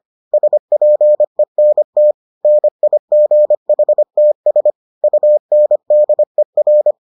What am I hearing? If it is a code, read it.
Morse code, heard as a keyed tone, about 25 wpm, 600 Hz. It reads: SPENT NIGHTS UNDER